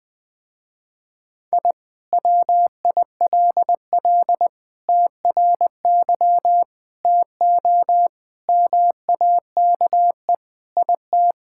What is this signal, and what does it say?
Morse code, heard as a keyed tone, about 20 wpm, 700 Hz. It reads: I WILL TRY TO MAKE IT
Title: Sentences with 3-Word Phrases - Spoken Beginning - 20wpm